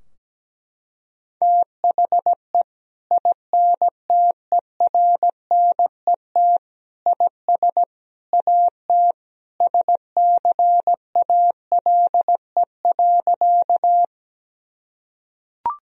Morse code, heard as a keyed tone, about 17 wpm, 700 Hz. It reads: THE INTERNET IS AT SCALE.
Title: Sentences from Top 1000 Words - Review Part 4 - 17wpm